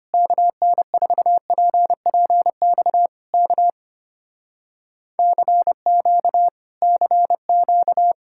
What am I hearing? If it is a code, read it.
Morse code, heard as a keyed tone, about 25 wpm, 700 Hz. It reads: KN4PPX K CQ CQ
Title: Speed-Racing Common QSO Phrases Encore - 25wpm